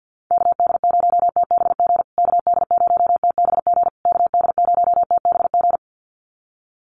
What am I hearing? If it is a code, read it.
Morse code, heard as a keyed tone, about 50 wpm, 700 Hz. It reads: KD0TBG KD0TBG KD0TBG
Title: USA Call Signs Repeated 3x 50wpm